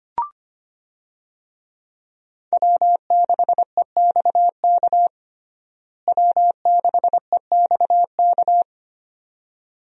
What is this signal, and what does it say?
Morse code, heard as a keyed tone, about 25 wpm, 700 Hz. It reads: W6EXK W6EXK
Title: USA Call Signs - 5 Characters - Repeated 2x - 25wpm